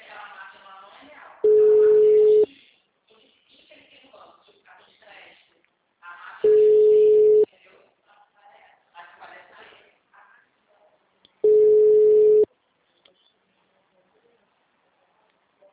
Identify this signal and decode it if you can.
Ringback tone (Europe, 425 Hz), with speech.